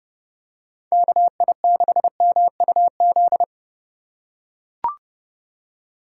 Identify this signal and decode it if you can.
Morse code, heard as a keyed tone, about 29 wpm, 700 Hz. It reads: KI6MUZ